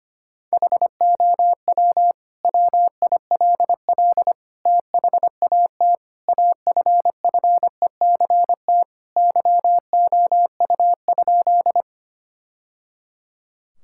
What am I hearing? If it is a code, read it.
Morse code, heard as a keyed tone, about 25 wpm, 700 Hz. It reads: HOW WILL THAT AFFECT YOU?
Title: Sentences from Top 1000 Words - Review Part 6 - 25wpm